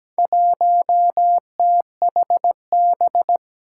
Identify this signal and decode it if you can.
Morse code, heard as a keyed tone, about 17 wpm, 700 Hz. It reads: 1THB
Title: Four Challenging Letter-Number Combinations 17wpm